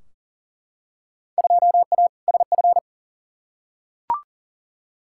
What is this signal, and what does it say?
Morse code, heard as a keyed tone, about 40 wpm, 700 Hz. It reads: 2A SF
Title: Field Day Exchanges - 40wpm